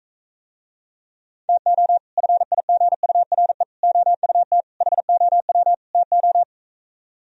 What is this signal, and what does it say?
Morse code, heard as a keyed tone, about 45 wpm, 700 Hz. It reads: TO FIGURE OUT HOW TO